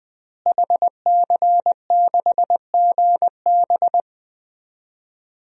Morse code, heard as a keyed tone, about 21 wpm, 700 Hz. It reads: HC6GB
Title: DXCC-100 Call Signs - 5 Characters - Repeated 2x - 20wpm